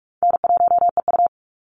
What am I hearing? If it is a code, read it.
Morse code, heard as a keyed tone, about 44 wpm, 700 Hz. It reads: N1EU
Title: NA SOTA Chaser Callsigns 45wpm